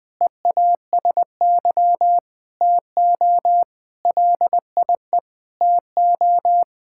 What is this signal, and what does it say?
Morse code, heard as a keyed tone, about 20 wpm, 700 Hz. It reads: EASY TO LIE TO
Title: Sentences from Top 1600 Words 20wpm